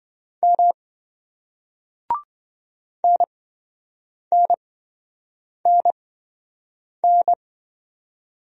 Morse code, heard as a keyed tone, about 27 wpm, spaced as for 7 wpm, 700 Hz. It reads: M NNNN